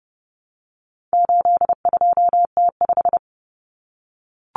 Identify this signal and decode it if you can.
Morse code, heard as a keyed tone, about 30 wpm, 700 Hz. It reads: 82T5
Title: Common Cut Numbers - 4-Digit 30wpm